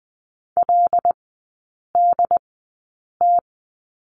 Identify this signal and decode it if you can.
Morse code, heard as a keyed tone, about 20 wpm, spaced as for 8 wpm, 700 Hz. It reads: LDT
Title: 074 Lesson 13e - Letter L - Three Characters 20wpm